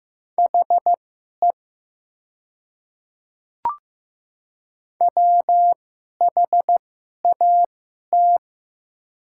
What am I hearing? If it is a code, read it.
Morse code, heard as a keyed tone, about 15 wpm, spaced as for 11 wpm, 690 Hz. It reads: HE WHAT
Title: 2x Character Spacing - Top 100 Words - 15wpm